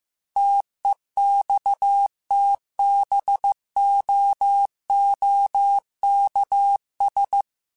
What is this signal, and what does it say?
Morse code, heard as a keyed tone, about 15 wpm, 800 Hz. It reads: TEXTBOOKS
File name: Technical Subjects / Technical Subjects-mp3s / Gates Intro @13